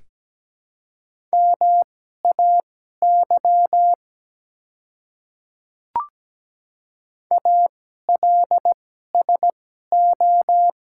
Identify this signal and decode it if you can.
Morse code, heard as a keyed tone, about 17 wpm, spaced as for 12 wpm, 700 Hz. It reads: MAY ALSO
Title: 2x Character Spacing - Top 200 Words - 17wpm